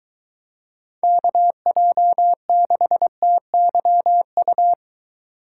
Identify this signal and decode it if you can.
Morse code, heard as a keyed tone, about 21 wpm, 700 Hz. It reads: KJ6TYU